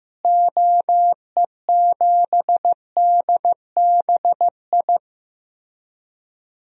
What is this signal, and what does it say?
Morse code, heard as a keyed tone, about 15 wpm, 700 Hz. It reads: OE7DBI
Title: DXCC-100 Call Signs - 6 Characters - Repeated 3x - 15wpm